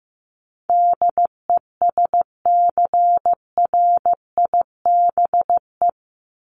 Morse code, heard as a keyed tone, about 15 wpm, 700 Hz. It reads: DESCRIBE
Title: ED Suffix - Top 100 Words with Base Words 15wpm